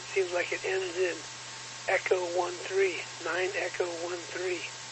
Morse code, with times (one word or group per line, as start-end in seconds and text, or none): none